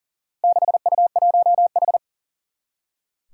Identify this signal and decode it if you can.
Morse code, heard as a keyed tone, about 38 wpm, 700 Hz. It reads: 6U1H
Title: Four Challenging Letter-Number Combinations 40wpm